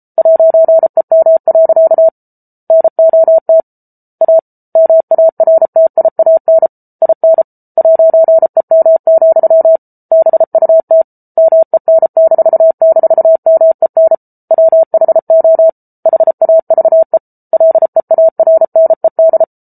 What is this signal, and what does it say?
Morse code, heard as a keyed tone, about 34 wpm, 650 Hz. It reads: 'EM. NOT A MARTIAN IN 'EM, BUT MEN--MEN WHO HAVE LEARNED